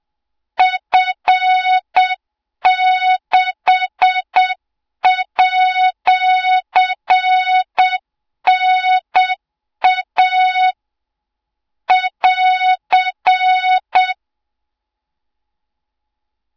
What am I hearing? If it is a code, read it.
Morse code, heard as a keyed tone, about 8 wpm, 3700 Hz, with noise. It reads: F6@NA <AR>
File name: Morze-HA2MN / hang / 20 GYAKORLAT